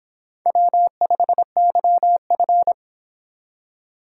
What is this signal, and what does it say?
Morse code, heard as a keyed tone, about 26 wpm, 700 Hz. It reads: W5YF